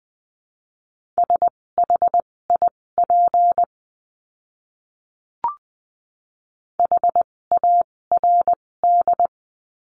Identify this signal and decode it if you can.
Morse code, heard as a keyed tone, about 20 wpm, spaced as for 16 wpm, 700 Hz. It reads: SHIP HARD
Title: 256 Lesson 42a - Words 12-20wpm